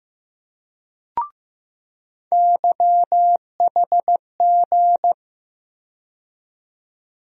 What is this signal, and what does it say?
Morse code, heard as a keyed tone, about 15 wpm, 700 Hz. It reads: YHG
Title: USA Call Sign Postfixes - 3 Letters 15wpm